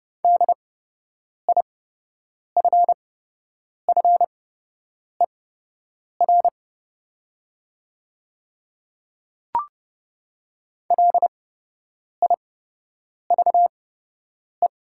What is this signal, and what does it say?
Morse code, heard as a keyed tone, about 30 wpm, spaced as for 8 wpm, 690 Hz. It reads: DIFFER LIVE
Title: 8x Character Spacing - Top 200 Words - 30wpm